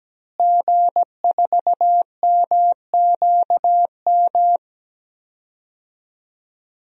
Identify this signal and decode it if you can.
Morse code, heard as a keyed tone, about 17 wpm, 700 Hz. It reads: G4MQM
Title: DXCC-100 Call Signs - 5 Characters - Repeated 3x - 17wpm